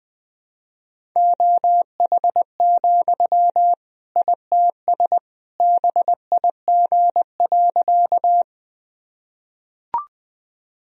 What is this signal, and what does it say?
Morse code, heard as a keyed tone, about 20 wpm, 700 Hz. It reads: OH, ITS BIG.
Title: Sentences from Top 1000 Words - Review Part 1 - 20wpm